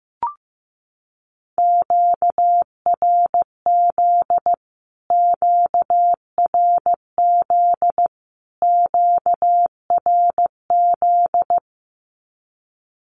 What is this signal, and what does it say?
Morse code, heard as a keyed tone, about 15 wpm, 700 Hz. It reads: QRZ QRZ QRZ
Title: Q-Codes Repeated 3x 15wpm